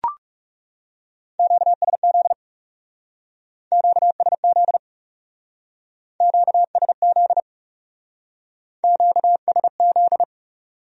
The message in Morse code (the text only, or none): QSZ QSZ QSZ QSZ